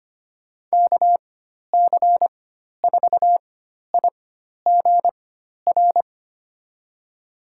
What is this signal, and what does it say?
Morse code, heard as a keyed tone, about 25 wpm, spaced as for 12 wpm, 700 Hz. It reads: KC4IGR
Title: USA Call Signs - 4x Character Spacing - 25wpm